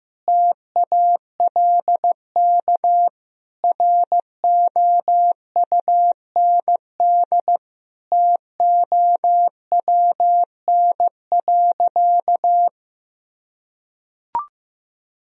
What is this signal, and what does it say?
Morse code, heard as a keyed tone, about 15 wpm, 700 Hz. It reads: TALK ROUND TOWN.
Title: Sentences from Top 1000 Words - Review Part 1 - 15wpm